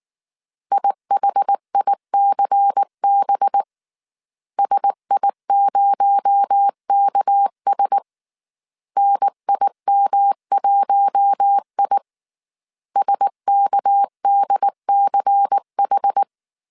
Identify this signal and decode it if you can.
Morse code, heard as a keyed tone, about 19 wpm, 800 Hz. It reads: IHICB SI0KS NIM1I SKDCH
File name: CW / DET / 1 6 1500C S A 18wpm 2255